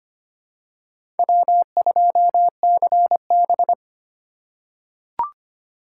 Morse code, heard as a keyed tone, about 25 wpm, 690 Hz. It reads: W2CB
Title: USA Call Signs - 4 Characters - 4x Character Spacing - 25wpm